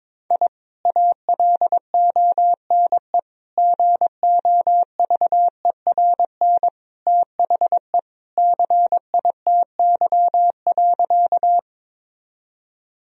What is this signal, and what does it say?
Morse code, heard as a keyed tone, about 22 wpm, 700 Hz. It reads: I ALONE GOVERN THE CITY.
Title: Sentences from Top 1000 Words - Review Part 2 - 22wpm